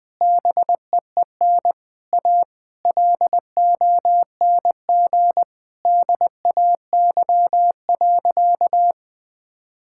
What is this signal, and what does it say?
Morse code, heard as a keyed tone, about 20 wpm, 700 Hz. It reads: BEEN A LONG DAY.
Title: Sentences with 3-Word Phrases - 20wpm